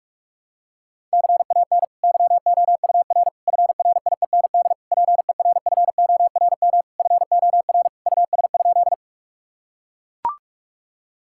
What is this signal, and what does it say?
Morse code, heard as a keyed tone, about 42 wpm, 700 Hz. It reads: CAN YOUR FRIEND PERFORM FOR US?
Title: Sentences from Top 1000 Words - Review Part 5 - 45wpm